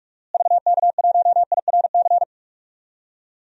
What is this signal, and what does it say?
Morse code, heard as a keyed tone, about 46 wpm, 700 Hz. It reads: VK1IRC
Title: DXCC-100 Call Signs - 6 Characters - Repeated 2x - 45wpm